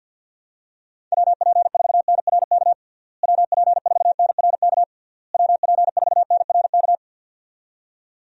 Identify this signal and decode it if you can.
Morse code, heard as a keyed tone, about 48 wpm, 700 Hz. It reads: WP4NRK WP4NRK WP4NRK
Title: USA Call Signs - 6 Characters - Repeated 3x - 50wpm